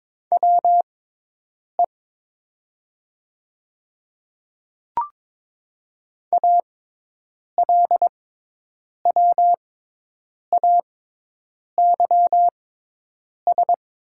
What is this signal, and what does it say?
Morse code, heard as a keyed tone, about 22 wpm, spaced as for 8 wpm, 700 Hz. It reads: WE ALWAYS
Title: 6x Character Spacing - Top 500 Words - 22wpm